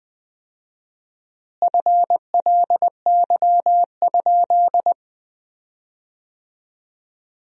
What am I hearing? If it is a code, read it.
Morse code, heard as a keyed tone, about 20 wpm, 700 Hz. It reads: FLY?
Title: 4x Word-Spacing - Sentences from Top 500 Words - 20wpm